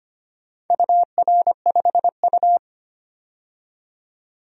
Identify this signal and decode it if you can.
Morse code, heard as a keyed tone, about 25 wpm, 700 Hz. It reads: UR5U
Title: DXCC-100 Call Signs - 4 Characters - 25wpm